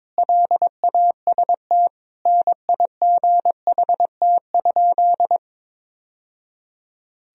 Morse code, heard as a keyed tone, about 22 wpm, 700 Hz. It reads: LAST NIGHT?
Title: Sentences from Top 1000 Words - Review Part 1 - 22wpm